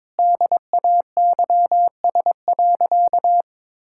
Morse code, heard as a keyed tone, about 22 wpm, 700 Hz. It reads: DAYS.
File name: Sentences with 7-Word Phrases - 22wpm